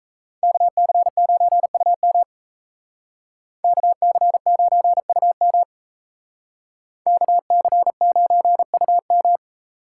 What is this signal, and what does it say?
Morse code, heard as a keyed tone, about 35 wpm, 700 Hz. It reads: KC9UM KC9UM KC9UM